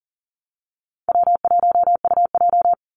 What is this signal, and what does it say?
Morse code, heard as a keyed tone, about 40 wpm, 700 Hz. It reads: W1UJ
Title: Call Signs - 40wpm